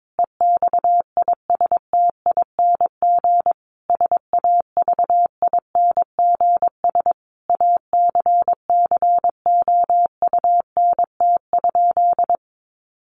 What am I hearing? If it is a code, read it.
Morse code, heard as a keyed tone, about 22 wpm, 700 Hz. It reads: EXISTING SAVINGS ACCOUNT?
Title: Sentences from Top 1500 Words 22wpm